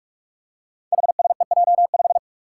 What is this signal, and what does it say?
Morse code, heard as a keyed tone, about 44 wpm, 700 Hz. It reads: HSEJ5
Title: Five Challenging Letter-Number Combinations 45wpm